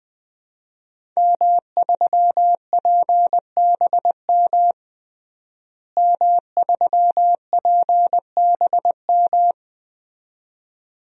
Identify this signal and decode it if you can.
Morse code, heard as a keyed tone, about 20 wpm, 700 Hz. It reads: M3PBM M3PBM
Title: DXCC-100 Call Signs - 5 Characters - Repeated 2x - 20wpm